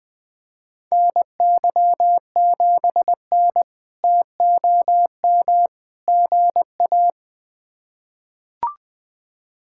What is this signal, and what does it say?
Morse code, heard as a keyed tone, about 20 wpm, 700 Hz. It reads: NY7N TOM GA
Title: K1USN SST QSO Practice - 20wpm